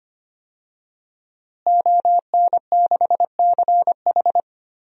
Morse code, heard as a keyed tone, about 25 wpm, 700 Hz. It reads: ON6CH
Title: DXCC-100 Call Signs - 5 Characters - Repeated 3x - 25wpm